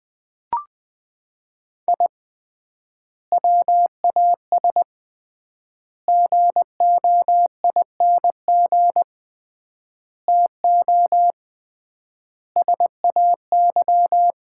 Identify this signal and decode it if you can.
Morse code, heard as a keyed tone, about 21 wpm, 700 Hz. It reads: I WAS GOING TO SAY
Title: 3x Word-Spacing - Sets of 5 Words - 20wpm